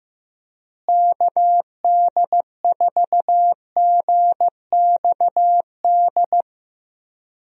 Morse code, heard as a keyed tone, about 15 wpm, 700 Hz. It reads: KD4GXD
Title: USA Call Signs - 6 Characters - Repeated 2x - 15wpm